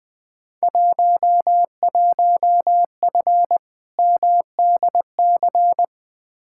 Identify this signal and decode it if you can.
Morse code, heard as a keyed tone, about 21 wpm, 700 Hz. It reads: 11F MDC
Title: Field Day Exchanges Encore - 20wpm